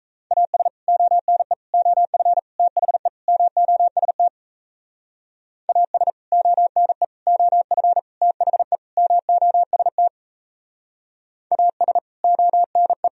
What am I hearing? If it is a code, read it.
Morse code, heard as a keyed tone, about 35 wpm, 700 Hz. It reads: AS ONE OF THE MOST AS ONE OF THE MOST AS ONE
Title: Speed-Racing - Sets of 5 Words 28wpm